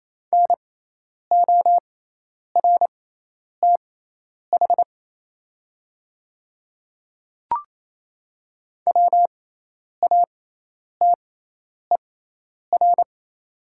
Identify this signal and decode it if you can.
Morse code, heard as a keyed tone, about 28 wpm, spaced as for 10 wpm, 700 Hz. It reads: NORTH WATER
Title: 6x Character Spacing - Top 500 Words - 28wpm